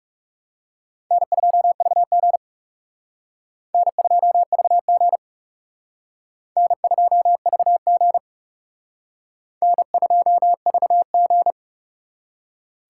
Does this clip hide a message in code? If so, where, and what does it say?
Yes: Morse code from 1.1 to 11.5 s, heard as a keyed tone, about 39 wpm, 700 Hz. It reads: N2VG N2VG N2VG N2VG